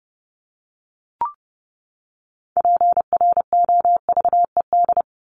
Morse code, heard as a keyed tone, about 30 wpm, 700 Hz. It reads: PROVED
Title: ED Suffix - Top 100 Words with Base Words 30wpm